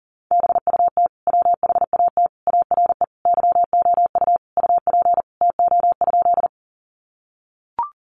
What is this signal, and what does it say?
Morse code, heard as a keyed tone, about 40 wpm, 700 Hz. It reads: BUT WHAT ARE YOU UP TO?
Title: Sentences from Top 100 Words 40wpm